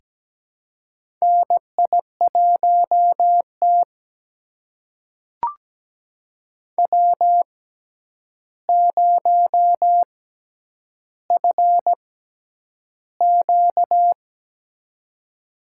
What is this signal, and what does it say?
Morse code, heard as a keyed tone, about 17 wpm, spaced as for 9 wpm, 700 Hz. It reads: NI1T W0FQ
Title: USA Call Signs - 4 Characters - 6x Character Spacing - 17wpm